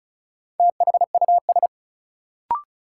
Morse code, heard as a keyed tone, about 34 wpm, 700 Hz. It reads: THUS
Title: Sentences from Top 800 Words - New Word Drill - 35wpm